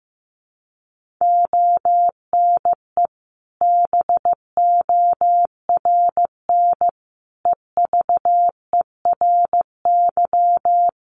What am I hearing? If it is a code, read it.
Morse code, heard as a keyed tone, about 14 wpm, 700 Hz. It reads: ONE BORN EVERY
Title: Sentences from Top 1400 Words 15wpm